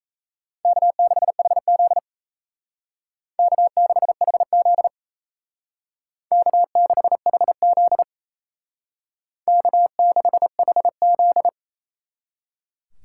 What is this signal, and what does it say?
Morse code, heard as a keyed tone, about 35 wpm, 700 Hz. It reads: K6HZ K6HZ K6HZ K6HZ
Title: Speed Racing - USA 1x2 Call Signs 28wpm